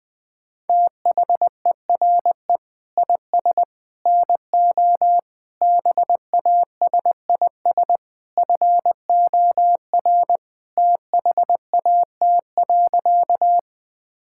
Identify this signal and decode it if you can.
Morse code, heard as a keyed tone, about 20 wpm, 700 Hz. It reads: THERE IS NO BASIS FOR THAT.
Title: Sentences from Top 1000 Words - Review Part 5 - 20wpm